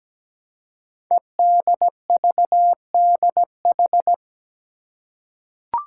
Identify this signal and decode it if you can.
Morse code, heard as a keyed tone, about 17 wpm, 700 Hz. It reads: EDVDH
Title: Code Groups - 5 Letters - 17wpm